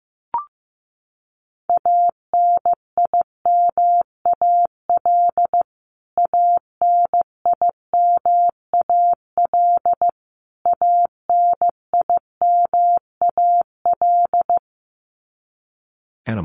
Morse code, heard as a keyed tone, about 15 wpm, 700 Hz. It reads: ANIMAL ANIMAL ANIMAL
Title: Top 300 Words Repeated 15wpm